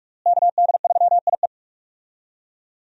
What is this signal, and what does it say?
Morse code, heard as a keyed tone, about 46 wpm, 700 Hz. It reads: KD3IE
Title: USA Call Signs - 5 Characters - Repeated 3x - 45wpm